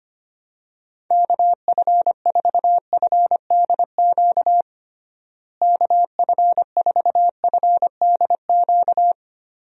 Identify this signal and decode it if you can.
Morse code, heard as a keyed tone, about 25 wpm, 700 Hz. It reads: KF4FDQ KF4FDQ
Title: USA Call Signs - 6 Characters - Repeated 3x - 25wpm